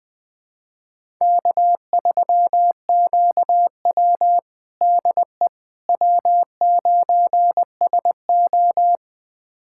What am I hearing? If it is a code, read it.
Morse code, heard as a keyed tone, about 20 wpm, 700 Hz. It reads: K3QW DE W9SO
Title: USA Callsign DE Callsign - 4 Characters - 20wpm